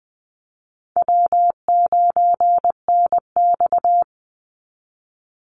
Morse code, heard as a keyed tone, about 20 wpm, 700 Hz. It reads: W9NX
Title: Easy Call Signs - 20wpm